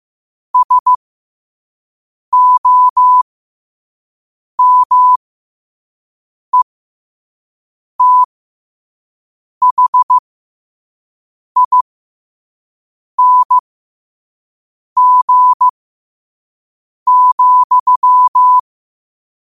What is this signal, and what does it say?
Morse code, heard as a keyed tone, about 15 wpm, spaced as for 5 wpm, 1000 Hz. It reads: SOMETHING,
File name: 2026-03-16-5wpm